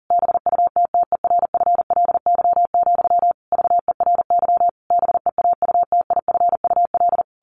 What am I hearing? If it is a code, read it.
Morse code, heard as a keyed tone, about 40 wpm, 700 Hz. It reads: BUTTERFLY, VERY BEAUTIFUL
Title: A Sound Of Thunder 40wpm